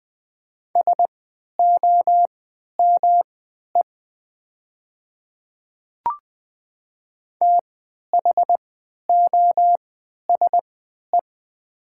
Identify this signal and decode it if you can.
Morse code, heard as a keyed tone, about 20 wpm, spaced as for 11 wpm, 700 Hz. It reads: SOME THOSE
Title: 3x Character Spacing - Top 500 Words - 20wpm